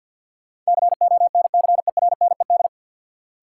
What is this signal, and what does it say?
Morse code, heard as a keyed tone, about 53 wpm, 700 Hz. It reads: CONCERNED
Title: Sentences from Top 1000 Words - Review Part 6 - New Word Drill - 50wpm